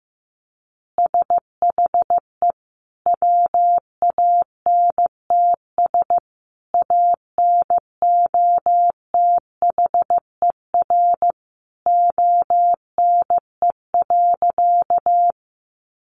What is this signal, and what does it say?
Morse code, heard as a keyed tone, about 15 wpm, 700 Hz. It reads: SHE WANTS ANOTHER ONE.
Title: Easy Phrases 15wpm